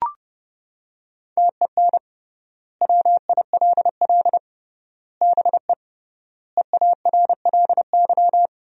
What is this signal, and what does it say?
Morse code, heard as a keyed tone, about 30 wpm, 700 Hz. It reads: TEN WILL BE EARLY
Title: 3x Word-Spacing - Sentences from Top 400 Words - 30wpm